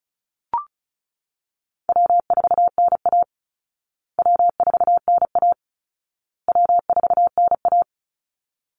Morse code, heard as a keyed tone, about 34 wpm, 700 Hz. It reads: W4NA W4NA W4NA
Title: Easy Call Signs Repeated 3x - 35wpm